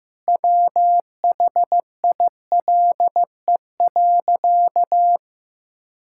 Morse code, heard as a keyed tone, about 15 wpm, 700 Hz. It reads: WHILE.
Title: Sentences from Top 1000 Words - Review Part 5 - 15wpm